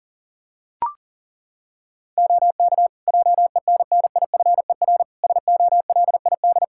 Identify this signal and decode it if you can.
Morse code, heard as a keyed tone, about 40 wpm, 700 Hz. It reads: OK JENNIFER SOLID
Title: Practice QSO - Exchange 3 - More Info - 40wpm